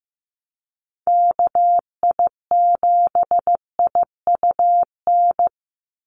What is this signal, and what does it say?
Morse code, heard as a keyed tone, about 15 wpm, 700 Hz. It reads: KI7IUN
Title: USA Call Signs 15wpm